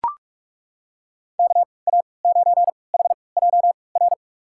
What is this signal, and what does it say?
Morse code, heard as a keyed tone, about 42 wpm, spaced as for 25 wpm, 700 Hz. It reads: KA9HJR
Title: USA Call Signs - 6 Characters - 3x Character Spacing - 45wpm